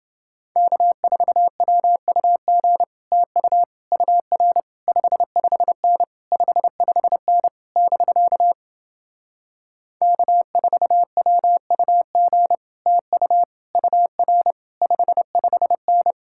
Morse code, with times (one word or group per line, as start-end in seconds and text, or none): 0.6-2.8 K4WUG
3.1-3.6 TU
3.9-4.6 UR
4.9-6.0 55N
6.3-7.5 55N
7.8-8.5 <BK>
10.0-12.6 K4WUG
12.9-13.4 TU
13.8-14.5 UR
14.8-16.1 55N